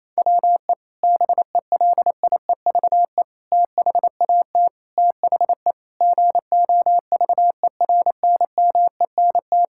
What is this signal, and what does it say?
Morse code, heard as a keyed tone, about 28 wpm, 700 Hz. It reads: WE BELIEVE THAT THE GOVERNMENT
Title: Sentences from Top 1000 Words - Review Part 6 - 28wpm